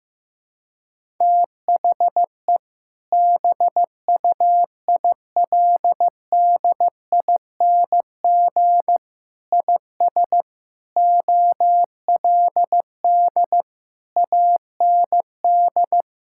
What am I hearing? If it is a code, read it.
Morse code, heard as a keyed tone, about 15 wpm, 700 Hz. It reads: THE BUILDING IS OLD AND
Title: Sentences from Top 1000 Words - Review Part 5 - 15wpm